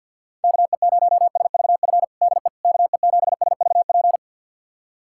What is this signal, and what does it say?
Morse code, heard as a keyed tone, about 51 wpm, 700 Hz. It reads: KE0SVF DE KE7SVP
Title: USA Callsign DE Callsign - 6 Characters - 50wpm